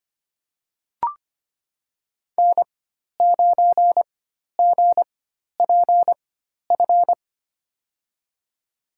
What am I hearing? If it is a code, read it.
Morse code, heard as a keyed tone, about 25 wpm, spaced as for 12 wpm, 700 Hz. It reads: N9GPF
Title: USA Call Signs - 4x Character Spacing - 25wpm